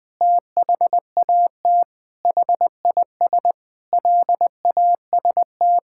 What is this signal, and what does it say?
Morse code, heard as a keyed tone, about 20 wpm, 700 Hz. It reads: THAT HIS LAST
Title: Sets of 9 Words - 20wpm